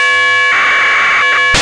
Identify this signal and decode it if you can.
Packet radio (AX.25, as in APRS).
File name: almasat 16h51 utc DK3WN